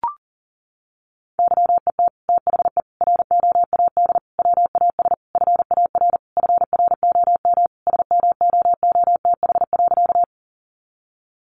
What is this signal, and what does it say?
Morse code, heard as a keyed tone, about 40 wpm, 700 Hz. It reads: YET THE ROAD WAS FAR FROM SMOOTH.
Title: Sentences from Top 1400 Words 40wpm